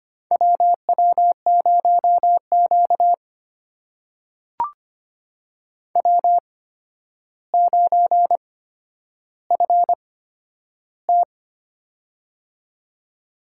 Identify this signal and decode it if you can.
Morse code, heard as a keyed tone, about 25 wpm, spaced as for 11 wpm, 700 Hz. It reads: WW0Q W9FT
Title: USA Call Signs - 8x Character Spacing - 25wpm